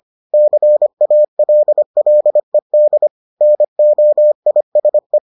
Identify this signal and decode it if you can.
Morse code, heard as a keyed tone, about 25 wpm, 600 Hz. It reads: CALLED NOISE